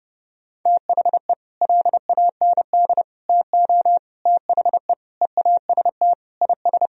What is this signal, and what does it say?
Morse code, heard as a keyed tone, about 28 wpm, 700 Hz. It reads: THE LAND TO THE EAST IS
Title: Sentences from Top 1000 Words - Review Part 4 - 30wpm